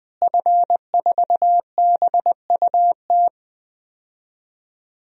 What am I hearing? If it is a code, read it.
Morse code, heard as a keyed tone, about 20 wpm, 700 Hz. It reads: F4BUT
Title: DXCC-100 Call Signs - 5 Characters - Repeated 3x - 20wpm